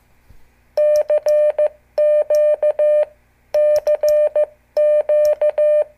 Morse code, heard as a keyed tone, about 14 wpm, 600 Hz. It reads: CQ CQ
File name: morse / qso / CQ